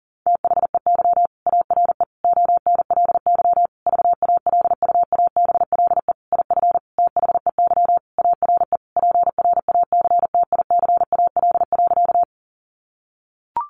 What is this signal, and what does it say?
Morse code, heard as a keyed tone, about 40 wpm, 700 Hz. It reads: THEY ARE ONLY VALUABLE IF THEY ARE PRACTICAL.
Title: Sentences from Top 1400 Words 40wpm